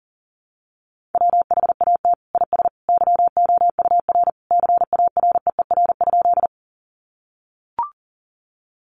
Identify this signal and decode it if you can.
Morse code, heard as a keyed tone, about 40 wpm, 700 Hz. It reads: WHAT IS YOUR CAREER?
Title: Sentences from Top 1000 Words 40wpm